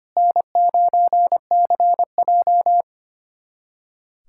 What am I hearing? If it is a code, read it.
Morse code, heard as a keyed tone, about 25 wpm, 700 Hz. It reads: N9CJ
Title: USA Callsign DE Callsign - 4 Characters - 25wpm